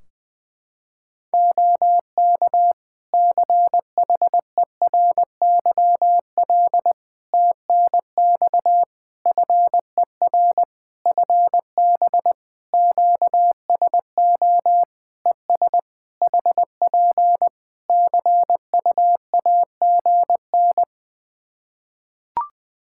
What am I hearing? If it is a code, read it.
Morse code, heard as a keyed tone, about 20 wpm, 700 Hz. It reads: OK CHERYL TNX FER FB QSO ES HP CUAGN
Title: Practice QSO - Exchange 4 - Ending - 20wpm